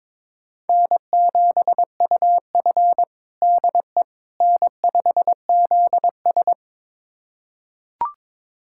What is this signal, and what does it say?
Morse code, heard as a keyed tone, about 22 wpm, 700 Hz. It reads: N7UF DE N5ZS
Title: USA Callsign DE Callsign - 4 Characters - 22wpm